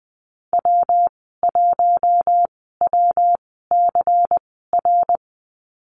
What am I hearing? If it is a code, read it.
Morse code, heard as a keyed tone, about 20 wpm, spaced as for 15 wpm, 700 Hz. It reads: W1WCR
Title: 147 Lesson 24g - Letter y - Call Signs 20wpm